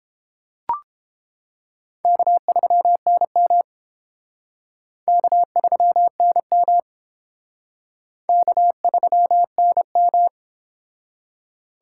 Morse code, heard as a keyed tone, about 28 wpm, 700 Hz. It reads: K3NM K3NM K3NM